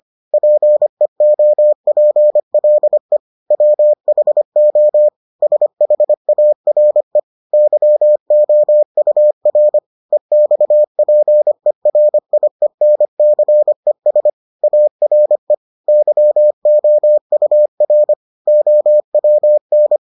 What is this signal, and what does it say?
Morse code, heard as a keyed tone, about 25 wpm, 600 Hz. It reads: PEOPLE WHO SHARE YOUR EXPERIENCES ARE YOUR OWN